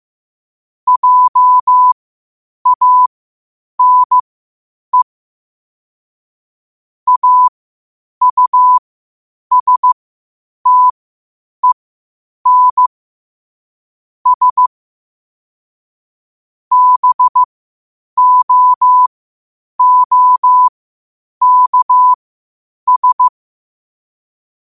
Morse code, heard as a keyed tone, about 15 wpm, spaced as for 8 wpm, 1000 Hz. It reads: JANE AUSTEN S BOOKS
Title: 8 WPM morse code quotes for Thu, 29 Feb 2024